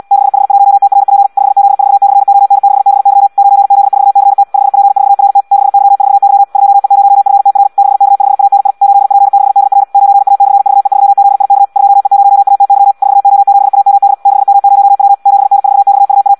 Morse code, heard as a keyed tone, about 74 wpm, 800 Hz. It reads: 6H1TRP 5Z4/ON6BY 9Z4CT 5P4RT 6Y4Y 2E0FTR 6K5ANI 8P6DR 9A/VE3ZIK 2E0ATY 4O7AMD 6M0W 7N4XUA
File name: Audio practica CW / qrq-tx-74 2